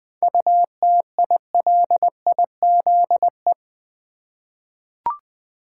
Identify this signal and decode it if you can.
Morse code, heard as a keyed tone, about 21 wpm, 700 Hz. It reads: UTILIZE
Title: 7 Letter Words - First 3 Letters Given - 20wpm